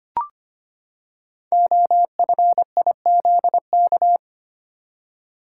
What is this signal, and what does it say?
Morse code, heard as a keyed tone, about 25 wpm, 700 Hz. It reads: OFIZK
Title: Code Groups - 5 Letters - 25wpm